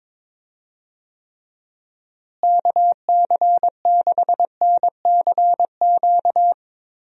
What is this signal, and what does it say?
Morse code, heard as a keyed tone, about 22 wpm, 700 Hz. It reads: KC6NCQ
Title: USA Call Signs - 6 Characters - Repeated 2x - 22wpm